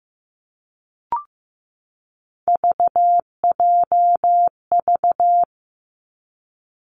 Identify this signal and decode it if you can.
Morse code, heard as a keyed tone, about 15 wpm, 700 Hz. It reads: VJV
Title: Three Challenging Letter-Number Combinations 15wpm